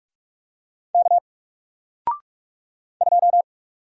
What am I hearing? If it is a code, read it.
Morse code, heard as a keyed tone, about 44 wpm, 700 Hz. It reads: K 2
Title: Single Letter-Number 45wpm